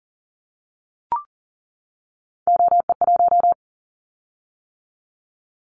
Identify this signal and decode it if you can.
Morse code, heard as a keyed tone, about 40 wpm, 700 Hz. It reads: OE1
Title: Callsign Prefixes for Top 50 DXCC - 40wpm